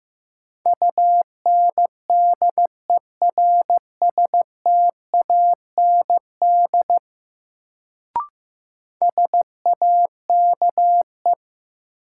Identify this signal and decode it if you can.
Morse code, heard as a keyed tone, about 15 wpm, 700 Hz. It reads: UNDERSTAND SAKE
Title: Sets of 4 Words - New Word Drill - 15wpm